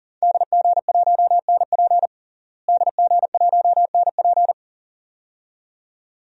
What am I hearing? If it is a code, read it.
Morse code, heard as a keyed tone, about 37 wpm, 700 Hz. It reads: DG1NP DG1NP
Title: DXCC-100 Call Signs - 5 Characters - Repeated 2x - 40wpm